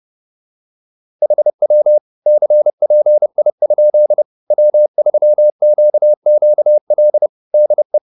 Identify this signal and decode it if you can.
Morse code, heard as a keyed tone, about 30 wpm, 600 Hz. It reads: HW CPI? W3QQL DE